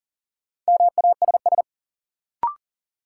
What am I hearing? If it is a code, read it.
Morse code, heard as a keyed tone, about 40 wpm, 700 Hz. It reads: MASS
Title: Sets of 5 Words Encore - New Word Drill - 40wpm